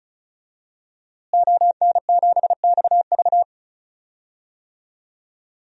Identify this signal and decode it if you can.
Morse code, heard as a keyed tone, about 34 wpm, 700 Hz. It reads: ON7XV
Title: DXCC-100 Call Signs - 5 Characters - Repeated 2x - 35wpm